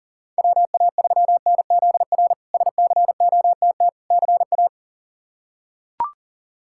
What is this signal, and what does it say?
Morse code, heard as a keyed tone, about 40 wpm, 700 Hz. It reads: WA3NZR SCOTT CA
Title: K1USN SST QSO Practice - 40wpm